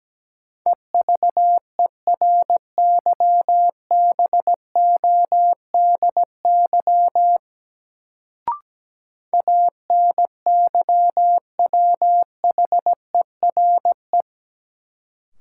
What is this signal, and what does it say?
Morse code, heard as a keyed tone, about 17 wpm, 700 Hz. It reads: EVERYBODY ANYWHERE
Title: Sentences from Top 1400 Words - New Word Drill - 17wpm